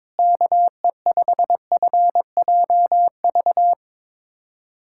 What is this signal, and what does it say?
Morse code, heard as a keyed tone, about 22 wpm, 700 Hz. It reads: KE5FJV
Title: USA Callsign DE Callsign - 6 Characters - 22wpm